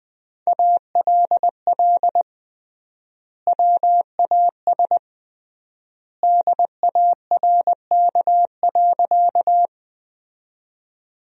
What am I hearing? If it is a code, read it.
Morse code, heard as a keyed tone, about 19 wpm, 700 Hz. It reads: ALL WAS DARK.
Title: 3x Word-Spacing - Sentences from Top 500 Words - 20wpm